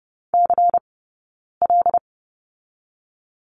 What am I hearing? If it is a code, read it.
Morse code, heard as a keyed tone, about 30 wpm, 700 Hz. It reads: C L
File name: Lesson 18d - Letter m - Two Characters 30wpm